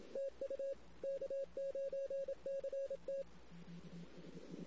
Morse code, heard as a keyed tone, about 27 wpm, 550 Hz. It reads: TU K9CT
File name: CW / 2008-ss-cw-k5tr-op-n5rz -16:00